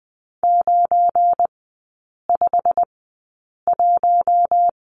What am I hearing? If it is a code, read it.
Morse code, heard as a keyed tone, about 20 wpm, spaced as for 8 wpm, 700 Hz. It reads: 951
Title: 172 Lesson 28e - Number 9 - Three Characters 20wpm